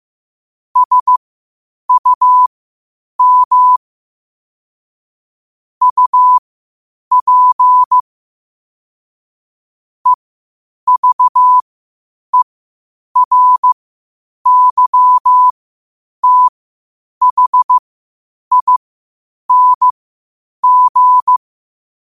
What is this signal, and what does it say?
Morse code, heard as a keyed tone, about 15 wpm, spaced as for 8 wpm, 1000 Hz. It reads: SUM UP EVERYTHING
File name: qotdfiles / 8wpm